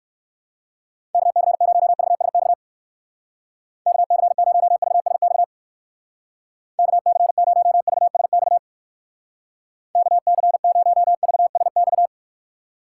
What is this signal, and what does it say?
Morse code, heard as a keyed tone, about 59 wpm, 700 Hz. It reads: KC0VSX KC0VSX KC0VSX KC0VSX